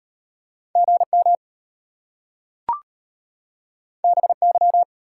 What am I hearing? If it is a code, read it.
Morse code, heard as a keyed tone, about 34 wpm, 700 Hz. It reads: GM BY